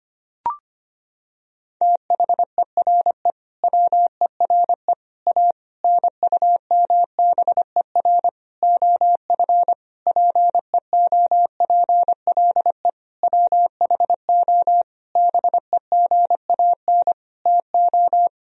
Morse code, heard as a keyed tone, about 25 wpm, 700 Hz. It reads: THERE WERE A NUMBER OF PEOPLE WHO BEGAN TO
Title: Sentences from Top 1000 Words - Review Part 1 - 25wpm